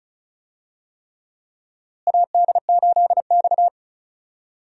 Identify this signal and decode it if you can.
Morse code, heard as a keyed tone, about 34 wpm, 700 Hz. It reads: AD8X
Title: USA Call Signs - 4 Characters - Repeated 3x - 35wpm